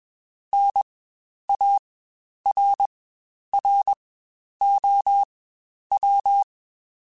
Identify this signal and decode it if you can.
Morse code, heard as a keyed tone, about 21 wpm, spaced as for 10 wpm, 800 Hz. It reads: NARROW
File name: Spaced / Chuckles Spaced-mp3s / MiddleAge sp @20